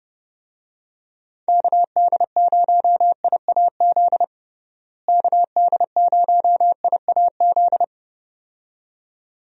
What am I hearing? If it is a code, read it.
Morse code, heard as a keyed tone, about 30 wpm, 700 Hz. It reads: KD0IAZ KD0IAZ
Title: USA Call Signs - 6 Characters - Repeated 2x - 30wpm